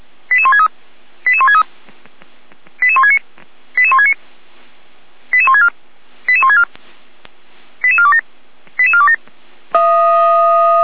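Five-tone selective call (ZVEI), heard as quick runs of five tones, tones of about 70 ms. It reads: 88163 88163 88169 88169 88164 88164 88427 88427
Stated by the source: ZVEI five-tone selective call 88203 received by SDR, logged by Feuerwehr Orlamünde.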